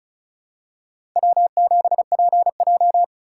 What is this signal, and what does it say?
Morse code, heard as a keyed tone, about 34 wpm, 700 Hz. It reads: W7PJ
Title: USA Call Signs - 4 Characters - 3x Character Spacing - 35wpm